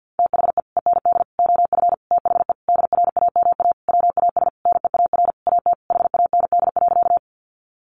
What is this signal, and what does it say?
Morse code, heard as a keyed tone, about 50 wpm, 700 Hz. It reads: THE END OF THE DRAMA WAS NEAR AT HAND.
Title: Sentences from Top 1500 Words 50wpm